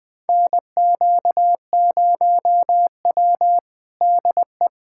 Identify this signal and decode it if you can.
Morse code, heard as a keyed tone, about 21 wpm, 700 Hz. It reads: NQ0W DE
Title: USA Callsign DE Callsign 20wpm